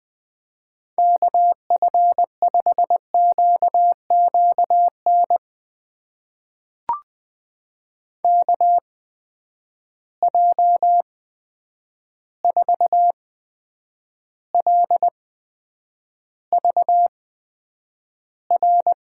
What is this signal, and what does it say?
Morse code, heard as a keyed tone, about 20 wpm, spaced as for 9 wpm, 700 Hz. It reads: KF5QQN KJ4LVR
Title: USA Call Signs - 6 Characters - 8x Character Spacing - 20wpm